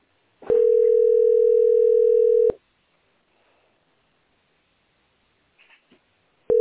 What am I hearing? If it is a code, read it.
Ringback tone (North America), 440 + 480 Hz.